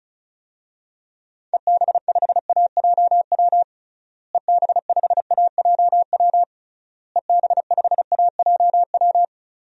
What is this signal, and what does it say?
Morse code, heard as a keyed tone, about 36 wpm, 700 Hz. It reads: EB5AJW EB5AJW EB5AJW
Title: DXCC-100 Call Signs - 6 Characters - Repeated 3x - 35wpm